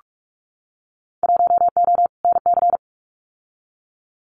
Morse code, heard as a keyed tone, about 46 wpm, 700 Hz. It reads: 1O NC
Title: Winter Field Day Exchanges 45wpm